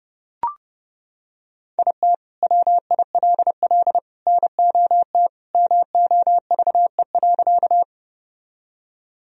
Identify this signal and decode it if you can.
Morse code, heard as a keyed tone, about 32 wpm, 700 Hz. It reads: IT WILL NOT MOVE.
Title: Sentences from Top 1000 Words - Review Part 1 - 30wpm